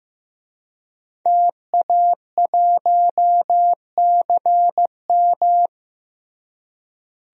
Morse code, heard as a keyed tone, about 15 wpm, 700 Hz. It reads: TA1CM
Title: DXCC-100 Call Signs - 5 Characters - Repeated 2x - 15wpm